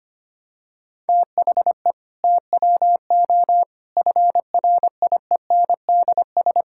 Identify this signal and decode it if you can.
Morse code, heard as a keyed tone, about 25 wpm, 700 Hz. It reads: THE TWO FRIENDS
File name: practice / 25wpm / Sentences with 2-Word Phrases - 25wpm